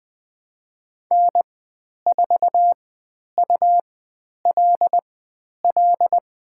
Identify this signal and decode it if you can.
Morse code, heard as a keyed tone, about 20 wpm, spaced as for 10 wpm, 700 Hz. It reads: N4ULL
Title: 102 Lesson 17g - Letter u - Call Signs 20wpm